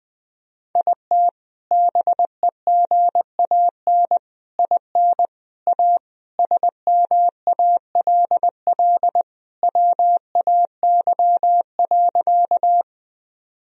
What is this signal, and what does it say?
Morse code, heard as a keyed tone, about 20 wpm, 700 Hz. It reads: IT BEGAN IN A SMALL WAY.
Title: Sentences from Top 1000 Words - Review Part 1 - 20wpm